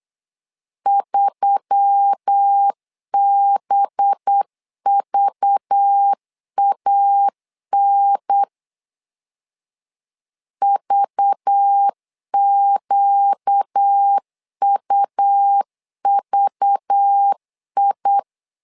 Morse code, heard as a keyed tone, about 8 wpm, 800 Hz. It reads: 3BVAN VQUVI
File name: LEZIONI / 5 LBVC / 5 LBV 8wpm 2837